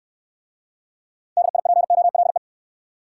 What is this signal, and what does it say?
Morse code, heard as a keyed tone, about 58 wpm, 700 Hz. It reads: DIPOLE